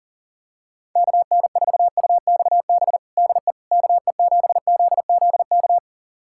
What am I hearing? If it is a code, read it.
Morse code, heard as a keyed tone, about 40 wpm, 700 Hz. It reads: KN4UXB DE KE7ZZK
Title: USA Callsign DE Callsign - 6 Characters - 40wpm